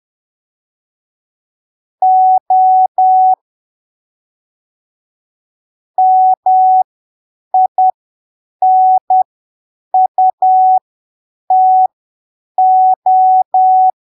Morse code, heard as a keyed tone, about 10 wpm, spaced as for 7 wpm, 750 Hz. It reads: O MINUTO
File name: arquivos / texto / texto 29 10ppm